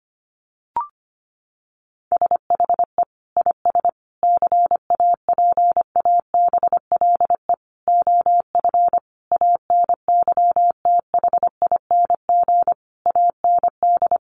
Morse code, heard as a keyed tone, about 25 wpm, 700 Hz. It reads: SHE IS CAPABLE OF ANYTHING AND
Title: Sentences from Top 1200 Words 25wpm